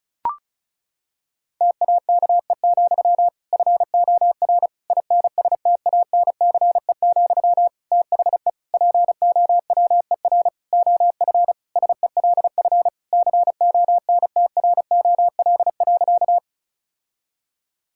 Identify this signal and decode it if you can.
Morse code, heard as a keyed tone, about 36 wpm, 690 Hz. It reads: TAKE, FOR INSTANCE, THE POWER OF SELF CONTROL.
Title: Sentences from Top 1000 Words - Review Part 5 - 35wpm